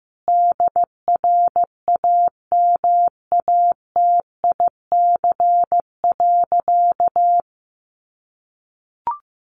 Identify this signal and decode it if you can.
Morse code, heard as a keyed tone, about 15 wpm, 700 Hz. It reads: DRAMATIC.
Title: Sentences from Top 1300 Words 15wpm